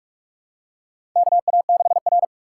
Morse code, heard as a keyed tone, about 44 wpm, 690 Hz. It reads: KA6R
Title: USA Call Signs - 4 Characters - 3x Character Spacing - 45wpm